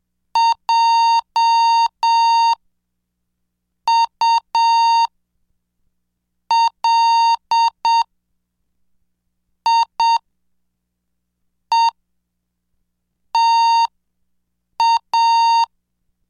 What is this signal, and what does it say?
Morse code, heard as a keyed tone, about 7 wpm, spaced as for 4 wpm, 920 Hz. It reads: JULIETA